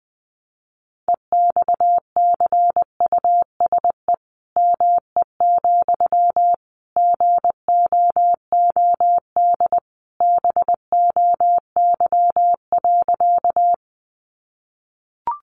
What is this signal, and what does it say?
Morse code, heard as a keyed tone, about 20 wpm, 700 Hz. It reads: EXCUSE ME, GOOD BOY.
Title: Sentences from Top 700 Words 20wpm